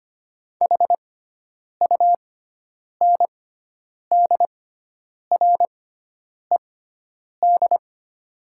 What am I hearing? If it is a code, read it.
Morse code, heard as a keyed tone, about 25 wpm, spaced as for 9 wpm, 700 Hz. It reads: HUNDRED
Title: 6x Character Spacing - Top 500 Words - 25wpm